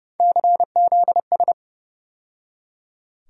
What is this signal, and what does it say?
Morse code, heard as a keyed tone, about 30 wpm, 700 Hz. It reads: CZS
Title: USA Call Sign Postfixes - 3 Letters 30wpm